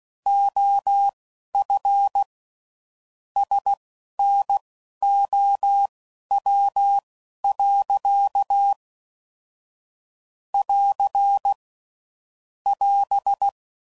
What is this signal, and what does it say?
Morse code, heard as a keyed tone, about 16 wpm, spaced as for 11 wpm, 800 Hz. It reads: OF SNOW. <AR> <AS>
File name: CW Code Groups @10WPM / Weather Sets / Weather 3 17